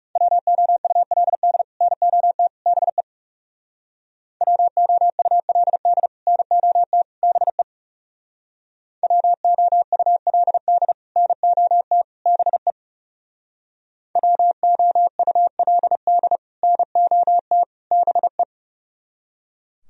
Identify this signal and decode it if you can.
Morse code, heard as a keyed tone, about 40 wpm, 700 Hz. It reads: WOULD NOT BE WOULD NOT BE WOULD NOT BE WOULD NOT BE